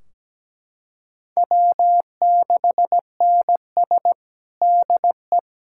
Morse code, heard as a keyed tone, about 17 wpm, 700 Hz. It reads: W6NS DE
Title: Practice QSO - Exchange 4 - Ending - 17wpm